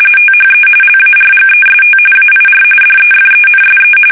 RTTY, about 45.45 baud. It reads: THE LARGEST AIR DEFENSE
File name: fsk850 / battleshipnj